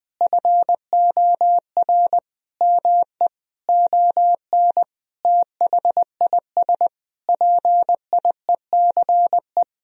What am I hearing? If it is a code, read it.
Morse code, heard as a keyed tone, about 19 wpm, 700 Hz. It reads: FOR ME ON THIS PIECE
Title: Sentences from Top 1000 Words - Review Part 2 - 20wpm